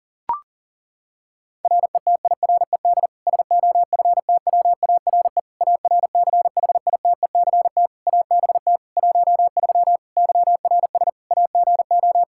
Morse code, heard as a keyed tone, about 40 wpm, 700 Hz. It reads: RETIRED SOFTWARE ARCHITECT ABT 13 YRS AGO
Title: Practice QSO - Exchange 3 - More Info - 40wpm